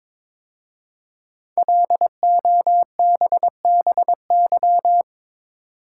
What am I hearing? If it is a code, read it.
Morse code, heard as a keyed tone, about 22 wpm, 700 Hz. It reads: LOBBY
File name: 5 Letter Words - First 3 Letters Given - 22wpm